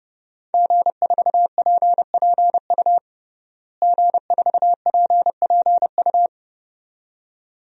Morse code, heard as a keyed tone, about 32 wpm, 700 Hz. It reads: G4PPU G4PPU
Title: DXCC-100 Call Signs - 5 Characters - Repeated 2x - 30wpm